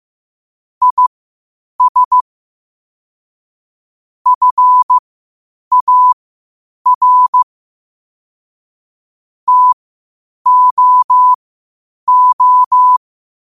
Morse code, heard as a keyed tone, about 15 wpm, spaced as for 8 wpm, 1000 Hz. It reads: IS FAR TOO
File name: qotdfiles / 8wpm